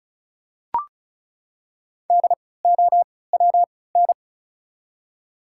Morse code, heard as a keyed tone, about 36 wpm, spaced as for 20 wpm, 700 Hz. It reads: DOWN
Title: 3x Character Spacing - Top 100 Words - 35wpm